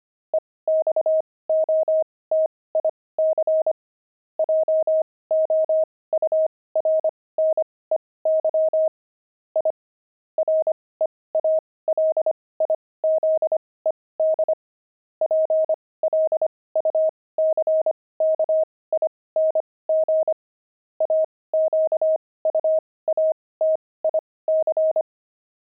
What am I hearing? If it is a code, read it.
Morse code, heard as a keyed tone, about 25 wpm, spaced as for 18 wpm, 630 Hz. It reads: EXOTIC JOURNEY I REALIZED PLUCKING AQUATIC
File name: pangram-fs-18-cs-25